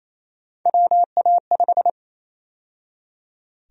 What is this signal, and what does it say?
Morse code, heard as a keyed tone, about 28 wpm, 700 Hz. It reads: WA5
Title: USA Call Sign Prefixes - 28wpm